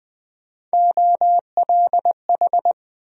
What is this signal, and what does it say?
Morse code, heard as a keyed tone, about 20 wpm, 700 Hz. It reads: OLH
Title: USA Call Sign Postfixes - 3 Letters 20wpm